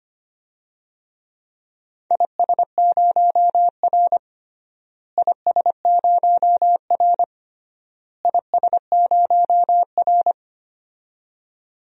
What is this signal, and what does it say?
Morse code, heard as a keyed tone, about 25 wpm, 700 Hz. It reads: IS0R IS0R IS0R
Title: DXCC-100 Call Signs - 4 Characters - Repeated 3x - 25wpm